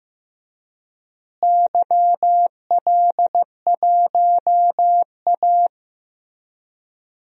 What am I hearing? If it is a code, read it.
Morse code, heard as a keyed tone, about 15 wpm, 700 Hz. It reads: YL1A
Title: DXCC-100 Call Signs - 4 Characters - Repeated 2x - 15wpm